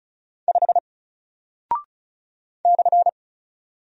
Morse code, heard as a keyed tone, about 35 wpm, 700 Hz. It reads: 5 /
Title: Single Character (Letter:Number:Punctuation:Prosign) Alt 35wpm